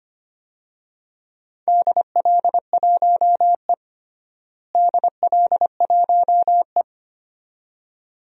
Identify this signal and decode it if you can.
Morse code, heard as a keyed tone, about 25 wpm, 700 Hz. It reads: DL1E DL1E
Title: DXCC-100 Call Signs - 4 Characters - Repeated 2x - 25wpm